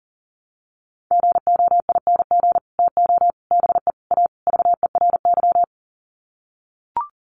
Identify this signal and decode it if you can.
Morse code, heard as a keyed tone, about 40 wpm, 700 Hz. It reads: GOING TO BE A VERY
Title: Sets of 5 Words Encore 40wpm